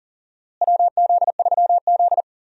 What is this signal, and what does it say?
Morse code, heard as a keyed tone, about 40 wpm, 700 Hz. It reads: WZ3Z
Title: USA Call Signs - 4 Characters - Repeated 2x - 40wpm